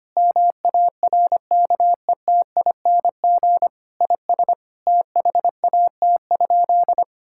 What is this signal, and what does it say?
Morse code, heard as a keyed tone, about 25 wpm, 700 Hz. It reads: MARKETING IS THAT?
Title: Sentences from Top 1000 Words - Review Part 5 - 25wpm